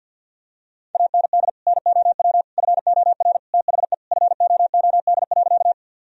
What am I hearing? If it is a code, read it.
Morse code, heard as a keyed tone, about 52 wpm, 700 Hz. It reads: AND NOW FOR THE FOOD.
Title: Sentences from Top 1000 Words - Review Part 1 - 50wpm